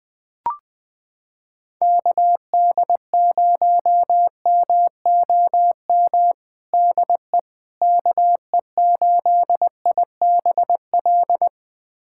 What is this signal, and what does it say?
Morse code, heard as a keyed tone, about 20 wpm, 700 Hz. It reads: KD0MOM DE KE8IBL
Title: USA Callsign DE Callsign - 6 Characters - 20wpm